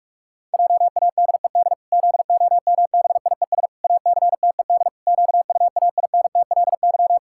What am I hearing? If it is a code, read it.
Morse code, heard as a keyed tone, about 41 wpm, 700 Hz. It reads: JADED ZOMBIES ACTED QUAINTLY
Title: Pangram Challenge v2 - 45wpm